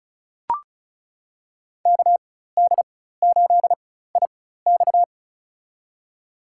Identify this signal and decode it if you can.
Morse code, heard as a keyed tone, about 36 wpm, spaced as for 16 wpm, 700 Hz. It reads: KD8IX
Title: USA Call Signs - 5 Characters - 4x Character Spacing - 35wpm